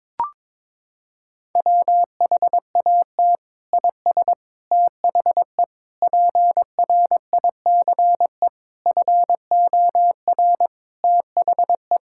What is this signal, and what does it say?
Morse code, heard as a keyed tone, about 21 wpm, 700 Hz. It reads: WHAT IS THE PRICE FOR THE
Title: Sentences from Top 1000 Words - Review Part 6 - 22wpm